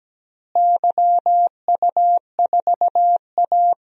Morse code, heard as a keyed tone, about 17 wpm, 700 Hz. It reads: YU4A
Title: DXCC-100 Call Signs - 4 Characters - Repeated 3x - 17wpm